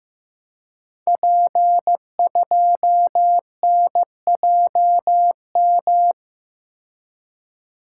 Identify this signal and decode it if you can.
Morse code, heard as a keyed tone, about 15 wpm, 700 Hz. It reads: P2NJM
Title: Code Groups - 5 Letter:Numbers - 15wpm